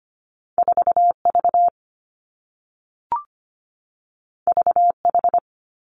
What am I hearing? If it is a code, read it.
Morse code, heard as a keyed tone, about 25 wpm, 700 Hz. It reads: 4 V V H
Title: Two Challenging Letter-Number Combinations 25wpm